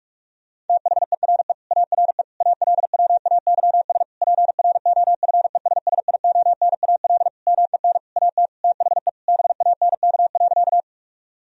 Morse code, heard as a keyed tone, about 46 wpm, 700 Hz. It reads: THERE ARE ALWAYS PROFESSIONAL MEN AT THE BANK.
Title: Sentences from Top 1000 Words - Review Part 4 - 45wpm